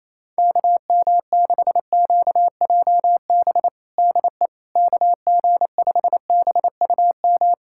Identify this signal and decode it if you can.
Morse code, heard as a keyed tone, about 27 wpm, 700 Hz. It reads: KM6QJB DE KG5BUM
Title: USA Callsign DE Callsign - 6 Characters - 28wpm